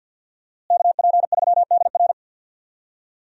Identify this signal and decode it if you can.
Morse code, heard as a keyed tone, about 50 wpm, 700 Hz. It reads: KP3DR
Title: USA Call Signs - 5 Characters - Repeated 2x - 50wpm